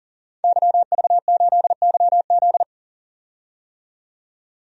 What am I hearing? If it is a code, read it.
Morse code, heard as a keyed tone, about 40 wpm, 700 Hz. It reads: YV8YZ
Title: DXCC-100 Call Signs - 5 Characters - 40wpm